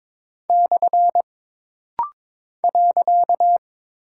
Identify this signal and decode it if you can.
Morse code, heard as a keyed tone, about 22 wpm, 700 Hz. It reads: / .
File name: practice / 22wpm / Single Punctuation-Prosign - Rapid-Fire - 22wpm